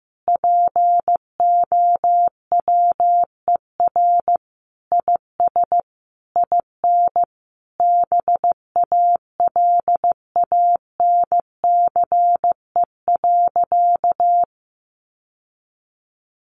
Morse code, heard as a keyed tone, about 15 wpm, 700 Hz. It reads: POWER IS IN BALANCE.
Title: Sentences from Top 900 Words 15wpm